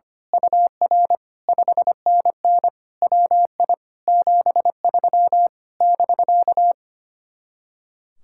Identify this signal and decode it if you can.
Morse code, heard as a keyed tone, about 24 wpm, 700 Hz. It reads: UR 5NN WI 73 <BK>
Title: Speed-Racing Common QSO Phrases Encore - 25wpm